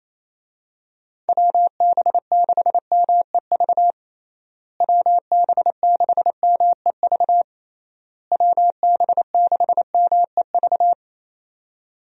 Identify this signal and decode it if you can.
Morse code, heard as a keyed tone, about 28 wpm, 700 Hz. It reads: WB6MEV WB6MEV WB6MEV
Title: USA Call Signs - 6 Characters - Repeated 3x - 28wpm